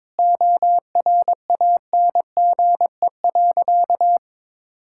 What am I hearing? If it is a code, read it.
Morse code, heard as a keyed tone, about 21 wpm, 700 Hz. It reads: ORANGE.
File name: practice / 22wpm / Easy Sentences with Hints - 22wpm